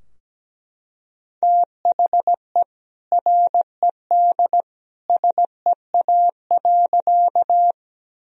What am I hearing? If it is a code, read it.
Morse code, heard as a keyed tone, about 17 wpm, 700 Hz. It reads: THE RED SEA.
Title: Sentences from Top 1000 Words - Review Part 2 - 17wpm